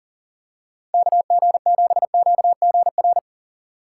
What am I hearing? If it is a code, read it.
Morse code, heard as a keyed tone, about 40 wpm, 700 Hz. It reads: KG7QGR
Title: USA Call Signs - 6 Characters - 6x Character Spacing - 40wpm